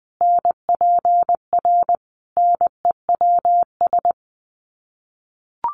Morse code, heard as a keyed tone, about 20 wpm, 700 Hz. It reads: NPR NEWS
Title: Sets of 5 Words Encore 20wpm